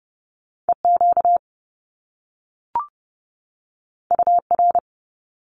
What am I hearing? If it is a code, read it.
Morse code, heard as a keyed tone, about 30 wpm, 700 Hz. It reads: EQ UR
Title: Top Two Letter Combinations in Words 30wpm